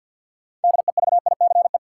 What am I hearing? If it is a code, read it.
Morse code, heard as a keyed tone, about 50 wpm, 700 Hz. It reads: DEVICE
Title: Sentences from Top 1500 Words - New Word Drill - 50wpm